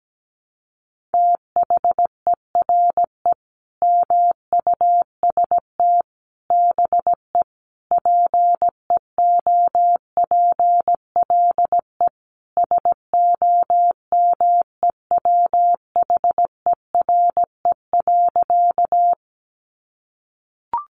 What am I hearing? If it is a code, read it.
Morse code, heard as a keyed tone, about 17 wpm, 700 Hz. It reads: THERE MUST BE PEOPLE SOMEWHERE.
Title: Sentences from Top 1400 Words 17wpm